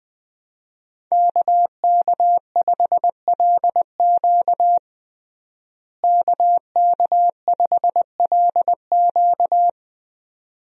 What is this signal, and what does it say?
Morse code, heard as a keyed tone, about 21 wpm, 700 Hz. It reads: KK5LQ KK5LQ
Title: USA Call Signs - 5 Characters - Repeated 2x - 20wpm